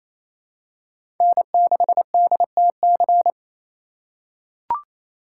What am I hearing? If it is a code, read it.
Morse code, heard as a keyed tone, about 28 wpm, 700 Hz. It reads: N6DTC
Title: USA Call Signs - 5 Characters - 6x Character Spacing - 28wpm